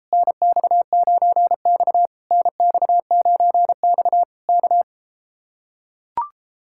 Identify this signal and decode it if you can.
Morse code, heard as a keyed tone, about 33 wpm, 700 Hz. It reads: NX9X NX9X K